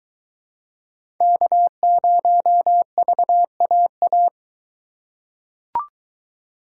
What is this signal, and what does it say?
Morse code, heard as a keyed tone, about 22 wpm, 700 Hz. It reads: K0VAA